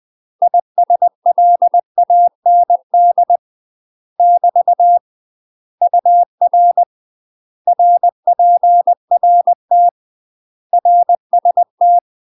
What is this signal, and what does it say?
Morse code, heard as a keyed tone, about 20 wpm, 700 Hz. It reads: ISLAND <BT> UR RPRT RST
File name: 20WPM